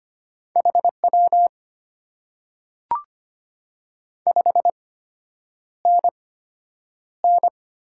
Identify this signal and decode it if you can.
Morse code, heard as a keyed tone, about 25 wpm, spaced as for 9 wpm, 700 Hz. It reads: HW 5NN
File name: Top 90 QSO Elements - 8x Character Spacing - 25wpm